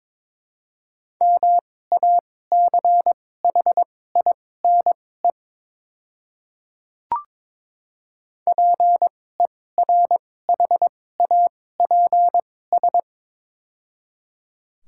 Morse code, heard as a keyed tone, about 22 wpm, spaced as for 16 wpm, 700 Hz. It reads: MACHINE PERHAPS
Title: 2x Character Spacing - Top 500 Words - 22wpm